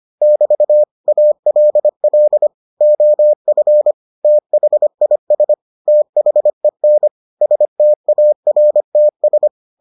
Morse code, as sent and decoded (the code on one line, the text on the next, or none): -...- / .- .-.. .-.. / --- ..-. / - .... .. ... / - .... -. / ... - .- .-. - ...
<BT> ALL OF THIS THEN STARTS